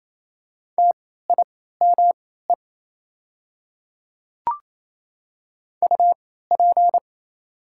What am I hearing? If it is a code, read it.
Morse code, heard as a keyed tone, about 28 wpm, spaced as for 16 wpm, 700 Hz. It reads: TIME UP
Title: 3x Character Spacing - Top 100 Words - 28wpm